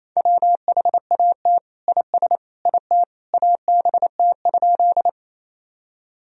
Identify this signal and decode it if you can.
Morse code, heard as a keyed tone, about 28 wpm, 700 Hz. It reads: WHAT IS IT ABT?
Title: Sentences from Top 100 Words and Abbreviations 28wpm